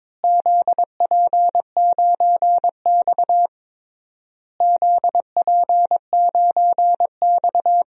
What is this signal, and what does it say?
Morse code, heard as a keyed tone, about 22 wpm, 700 Hz. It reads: ZP9X ZP9X
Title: DXCC-100 Call Signs - 4 Characters - Repeated 3x - 22wpm